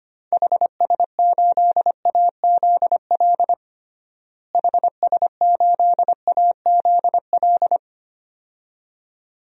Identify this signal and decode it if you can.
Morse code, heard as a keyed tone, about 25 wpm, 700 Hz. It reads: HS8AZL HS8AZL
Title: DXCC-100 Call Signs - 6 Characters - Repeated 2x - 25wpm